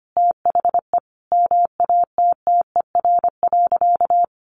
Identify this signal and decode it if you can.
Morse code, heard as a keyed tone, about 25 wpm, 700 Hz. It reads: THE MATTER.
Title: Sentences from Top 700 Words 25wpm